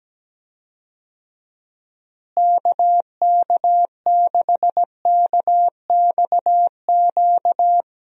Morse code, heard as a keyed tone, about 17 wpm, 700 Hz. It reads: KK6KXQ
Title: USA Call Signs - 6 Characters - Repeated 2x - 17wpm